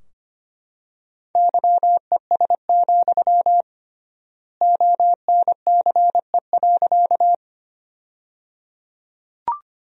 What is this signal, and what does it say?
Morse code, heard as a keyed tone, about 25 wpm, 700 Hz. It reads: YES, ONCE.
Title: 3x Word-Spacing - Sentences from Top 500 Words - 25wpm